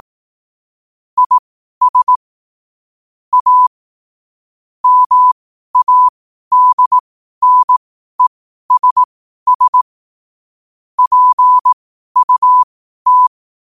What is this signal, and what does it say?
Morse code, heard as a keyed tone, about 18 wpm, spaced as for 12 wpm, 1000 Hz. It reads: IS A MADNESS PUT
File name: qotdfiles / 12wpm